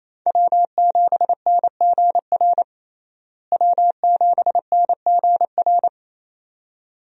Morse code, heard as a keyed tone, about 28 wpm, 700 Hz. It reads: W7NGR W7NGR
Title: USA Call Signs - 5 Characters - Repeated 2x - 28wpm